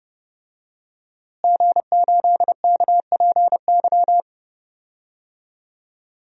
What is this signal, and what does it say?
Morse code, heard as a keyed tone, about 31 wpm, 700 Hz. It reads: G8KPY
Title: DXCC-100 Call Signs - 5 Characters - Repeated 2x - 30wpm